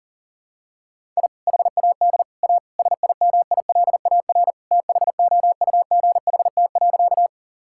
Morse code, heard as a keyed tone, about 40 wpm, 700 Hz. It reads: I HAD A SIMILAR THOUGHT.
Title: Sentences from Top 1000 Words - Review Part 5 - 40wpm